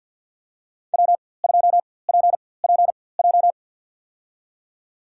Morse code, heard as a keyed tone, about 52 wpm, spaced as for 24 wpm, 700 Hz. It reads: W2PPJ
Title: USA Call Signs - 4x Character Spacing - 50wpm